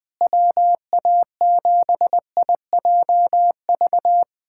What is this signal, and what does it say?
Morse code, heard as a keyed tone, about 20 wpm, 700 Hz. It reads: WA7IJV
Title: USA Callsign DE Callsign - 6 Characters - 20wpm